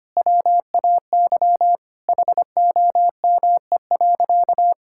Morse code, heard as a keyed tone, about 25 wpm, 700 Hz. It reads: WAY HOME.
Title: Sentences from Top 1600 Words 25wpm